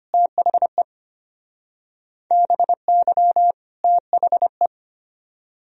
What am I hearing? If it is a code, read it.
Morse code, heard as a keyed tone, about 32 wpm, 700 Hz. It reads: THE BY THE